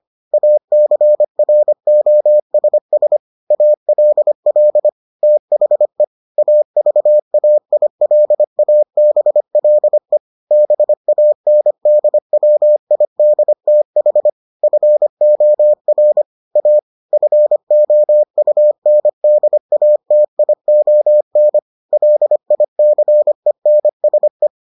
Morse code, heard as a keyed tone, about 25 wpm, 600 Hz. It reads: ACROSS ALL THE AVAILABLE BANDWIDTH FOR A FOUNDATION LICENSE